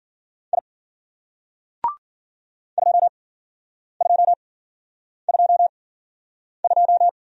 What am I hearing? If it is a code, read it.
Morse code, heard as a keyed tone, about 54 wpm, spaced as for 9 wpm, 700 Hz. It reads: I 2222